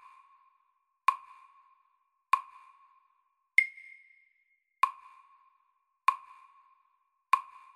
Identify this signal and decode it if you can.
There is no code.